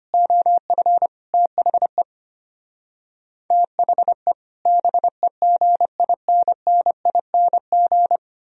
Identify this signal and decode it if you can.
Morse code, heard as a keyed tone, about 31 wpm, 690 Hz. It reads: OF THE THE BEGINNING
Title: Speed-Racing - Sets of 4 Words 25wpm